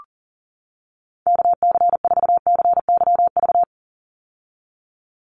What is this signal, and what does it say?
Morse code, heard as a keyed tone, about 40 wpm, 700 Hz. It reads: KC4CYV
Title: USA Call Signs 40wpm